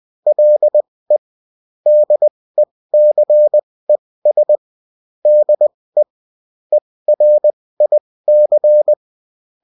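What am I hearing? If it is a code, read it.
Morse code, heard as a keyed tone, about 20 wpm, spaced as for 16 wpm, 600 Hz. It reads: LE DECES DE ERIC
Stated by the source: Bulletin in text and CW form.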